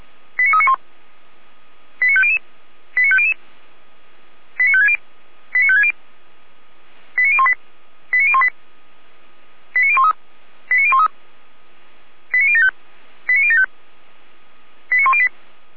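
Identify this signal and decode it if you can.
Five-tone selective call (ZVEI), heard as quick runs of five tones, tones of about 80 ms. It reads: 89391 89500 89500 89560 89560 89018 89018 89013 89013 89075 89075 89108
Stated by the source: ZVEI five-tone selective call 89350 received by SDR.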